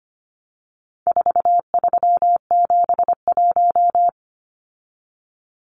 Morse code, heard as a keyed tone, about 25 wpm, 700 Hz. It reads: 4371
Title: Numbers - 4-Digit 25wpm